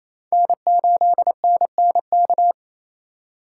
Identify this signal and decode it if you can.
Morse code, heard as a keyed tone, about 27 wpm, 700 Hz. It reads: N8NNK
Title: USA Call Signs - 2x Character Spacing - 28wpm